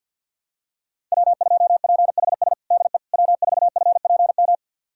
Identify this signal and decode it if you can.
Morse code, heard as a keyed tone, about 47 wpm, 700 Hz. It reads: W2PHS DE W4FPM
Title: USA Callsign DE Callsign - 5 Characters - 50wpm